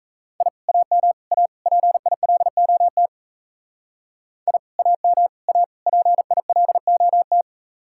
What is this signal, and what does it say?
Morse code, heard as a keyed tone, about 35 wpm, 700 Hz. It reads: I AM A PILOT I AM A PILOT
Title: Speed-Racing Common QSO Phrases - 28wpm